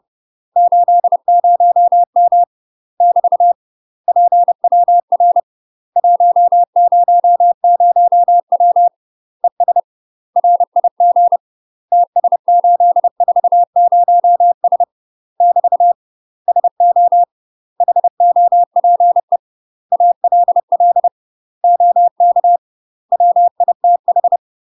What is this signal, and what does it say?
Morse code, heard as a keyed tone, about 30 wpm, 700 Hz. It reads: 80M <BT> PWR 100W ES RIG TS840S <BT> SO HOPE ALL OK WITH
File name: UK MASTER QSOs TRAINING / 30WPM / UK MASTER QSOs 0027